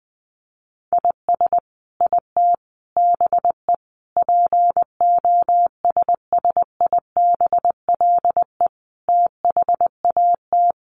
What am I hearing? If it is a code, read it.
Morse code, heard as a keyed tone, about 20 wpm, 700 Hz. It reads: IS IT BE POSSIBLE THAT
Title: Sentences from Top 1200 Words 20wpm